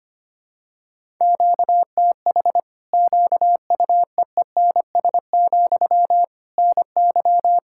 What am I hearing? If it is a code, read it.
Morse code, heard as a keyed tone, about 25 wpm, 700 Hz. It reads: QTH QUEENS, NY
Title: Practice QSO - Exchange 1 - Beginning - 25wpm